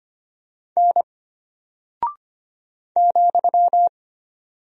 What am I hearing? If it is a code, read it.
Morse code, heard as a keyed tone, about 25 wpm, 700 Hz. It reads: N ,
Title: Single Character (Letter:Number:Punctuation:Prosign) Alt 25wpm